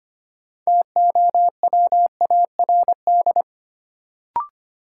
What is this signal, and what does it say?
Morse code, heard as a keyed tone, about 25 wpm, 700 Hz. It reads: TOWARD
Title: Sentences from Top 1000 Words - Review Part 6 - New Word Drill - 25wpm